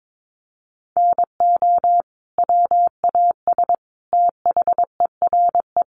Morse code, heard as a keyed tone, about 22 wpm, 700 Hz. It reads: NO WAS THERE
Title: Sentences from Top 100 Words 22wpm